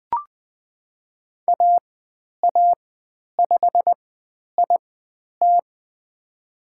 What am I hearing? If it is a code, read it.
Morse code, heard as a keyed tone, about 20 wpm, spaced as for 10 wpm, 700 Hz. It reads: AA5IT
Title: 109 Lesson 18g - Letter m - Call Signs 20wpm